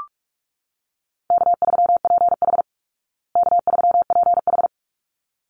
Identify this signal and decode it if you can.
Morse code, heard as a keyed tone, about 44 wpm, 700 Hz. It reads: K3PH K3PH
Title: Easy Call Signs Repeated 2x - 45wpm